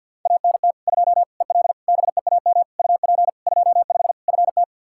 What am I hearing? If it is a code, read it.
Morse code, heard as a keyed tone, about 51 wpm, 700 Hz. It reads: ANT 2 EL BEAM UP 25 FT
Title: K1USN SST QSO Practice 50wpm